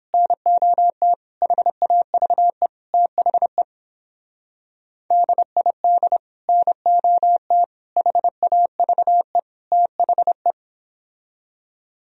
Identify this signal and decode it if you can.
Morse code, heard as a keyed tone, about 31 wpm, 700 Hz. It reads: NOT HAVE THE DID NOT HAVE THE